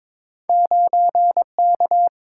Morse code, heard as a keyed tone, about 22 wpm, 700 Hz. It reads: 9K
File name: practice / 22wpm / Two Letter-Number Combinations - 22wpm